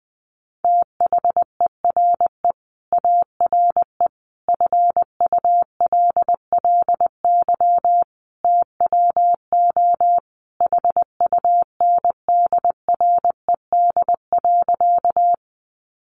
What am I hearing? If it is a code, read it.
Morse code, heard as a keyed tone, about 20 wpm, 700 Hz. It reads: THERE ARE FULLY TWO HUNDRED.
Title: Sentences from Top 1300 Words 20wpm